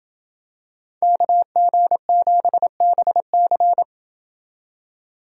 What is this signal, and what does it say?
Morse code, heard as a keyed tone, about 27 wpm, 700 Hz. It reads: KG7BC